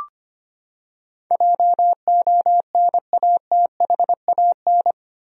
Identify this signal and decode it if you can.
Morse code, heard as a keyed tone, about 25 wpm, 700 Hz. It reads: JONATHAN
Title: USA Common Names - 25wpm